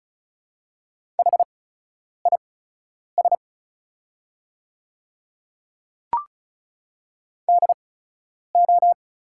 Morse code, heard as a keyed tone, about 36 wpm, spaced as for 10 wpm, 700 Hz. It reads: HIS DO
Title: 8x Character Spacing - Top 100 Words - 35wpm